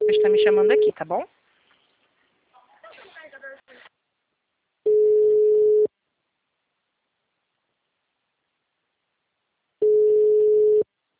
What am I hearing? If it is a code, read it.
Ringback tone (Europe, 425 Hz), with speech.